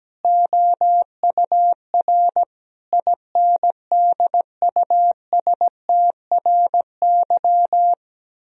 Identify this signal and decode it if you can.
Morse code, heard as a keyed tone, about 17 wpm, 700 Hz. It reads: OUR INDUSTRY
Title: Sentences from Top 1000 Words - Review Part 4 - 17wpm